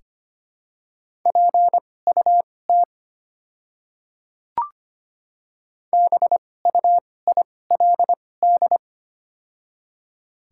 Morse code, heard as a keyed tone, about 25 wpm, spaced as for 18 wpm, 700 Hz. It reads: PUT BUILD
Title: 2x Character Spacing - Top 300 Words - 25wpm